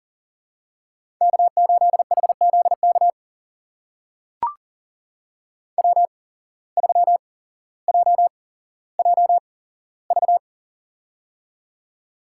Morse code, heard as a keyed tone, about 40 wpm, spaced as for 17 wpm, 700 Hz. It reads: K8HZK W3JJV
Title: USA Call Signs - 5 Characters - 8x Character Spacing - 40wpm